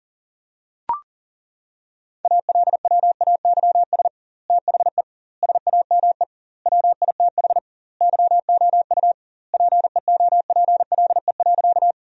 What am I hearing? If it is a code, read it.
Morse code, heard as a keyed tone, about 40 wpm, 700 Hz. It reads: ALWAYS THE SAME WITH YOU PEOPLE.
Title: 2x Word-Spacing - Sentences from Top 300 Words - 40wpm